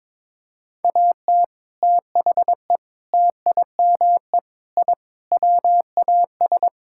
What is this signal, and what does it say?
Morse code, heard as a keyed tone, about 22 wpm, 700 Hz. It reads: AT THE TIME I WAS
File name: practice / 22wpm / Sentences with 3-Word Phrases - 22wpm